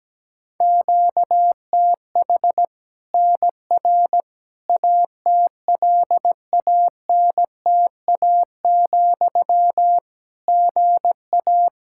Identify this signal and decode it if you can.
Morse code, heard as a keyed tone, about 17 wpm, 700 Hz. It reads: QTH NR ATLANTA, GA
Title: K1USN SST QSO Practice 17wpm